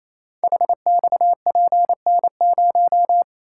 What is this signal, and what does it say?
Morse code, heard as a keyed tone, about 28 wpm, 700 Hz. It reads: HXPN0
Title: Code Groups - 5 Letter:Numbers - 28wpm